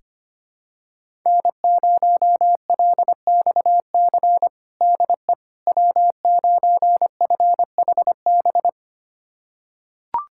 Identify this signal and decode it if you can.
Morse code, heard as a keyed tone, about 25 wpm, 700 Hz. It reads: N0LXC DE W9FHB
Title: USA Callsign DE Callsign - 5 Characters - 25wpm